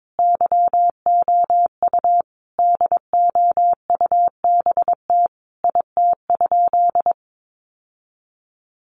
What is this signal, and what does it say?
Morse code, heard as a keyed tone, about 22 wpm, 700 Hz. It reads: YOU DOUBT IT?
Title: Sentences from Top 1500 Words 22wpm